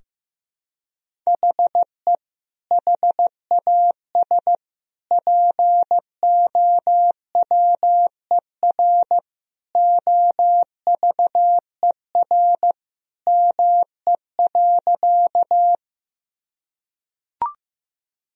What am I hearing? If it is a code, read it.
Morse code, heard as a keyed tone, about 15 wpm, 700 Hz. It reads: HE HAS POWER OVER ME.
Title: Sentences from Top 1000 Words - Review Part 2 - 15wpm